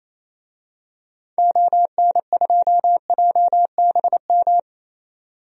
Morse code, heard as a keyed tone, about 28 wpm, 700 Hz. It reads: ON2JBM
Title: DXCC-100 Call Signs - 6 Characters - 28wpm